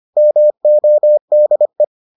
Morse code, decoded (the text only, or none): MODE